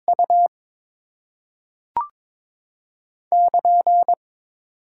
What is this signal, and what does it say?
Morse code, heard as a keyed tone, about 22 wpm, 700 Hz. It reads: U <KN>
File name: Top 90 QSO Elements - 8x Character Spacing - 22wpm